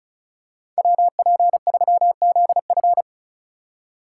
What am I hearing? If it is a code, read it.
Morse code, heard as a keyed tone, about 34 wpm, 700 Hz. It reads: WP3ZF
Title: USA Call Signs - 5 Characters - 3x Character Spacing - 35wpm